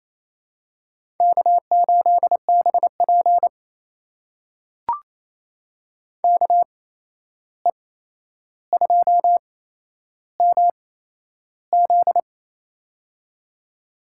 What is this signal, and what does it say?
Morse code, heard as a keyed tone, about 28 wpm, spaced as for 11 wpm, 700 Hz. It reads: K8BP KE2MZ
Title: USA Call Signs - 8x Character Spacing - 28wpm